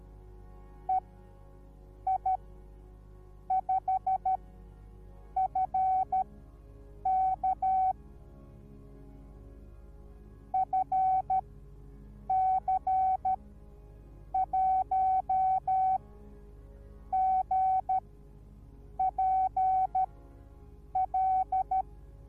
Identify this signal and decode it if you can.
Morse code, heard as a keyed tone, about 13 wpm, spaced as for 6 wpm, 760 Hz. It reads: EI5FK FC1GPL